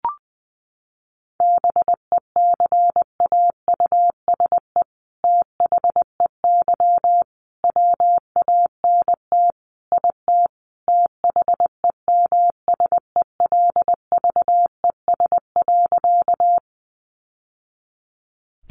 Morse code, heard as a keyed tone, about 20 wpm, 700 Hz. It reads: BECAUSE THEY WANT IT THEMSELVES.
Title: Sentences from Top 800 Words 20wpm